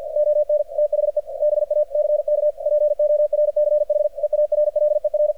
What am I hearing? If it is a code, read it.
Morse code, heard as a keyed tone, about 50 wpm, 600 Hz, with radio noise. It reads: ON THE BACK PORCH EARLER
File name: archived / CRSnet